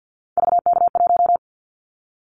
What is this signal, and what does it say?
Morse code, heard as a keyed tone, about 50 wpm, 700 Hz. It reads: VK1
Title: Callsign Prefixes for Top 50 DXCC - 50wpm